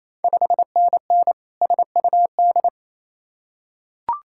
Morse code, heard as a keyed tone, about 28 wpm, 700 Hz. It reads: 5NN SUD
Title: Ontario QSO Party Exchange 28wpm